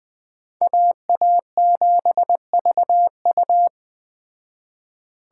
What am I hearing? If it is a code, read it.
Morse code, heard as a keyed tone, about 20 wpm, 700 Hz. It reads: AA7VU
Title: Speed Racing - USA 2x2 Call Signs 17wpm